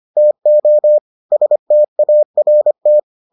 Morse code, heard as a keyed tone, about 25 wpm, 600 Hz. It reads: TO START